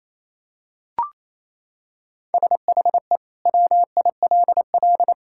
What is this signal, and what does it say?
Morse code, heard as a keyed tone, about 28 wpm, 700 Hz. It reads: SHE WILL
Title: Sentences from Top 1600 Words 28wpm